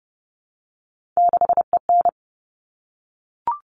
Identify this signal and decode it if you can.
Morse code, heard as a keyed tone, about 30 wpm, 700 Hz. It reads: 6EN
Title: All Cut Numbers - 3-Digit 30wpm